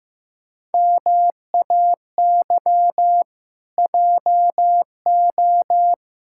Morse code, heard as a keyed tone, about 15 wpm, 700 Hz. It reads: MAY JO
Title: Pangram Challenge v2 - 15wpm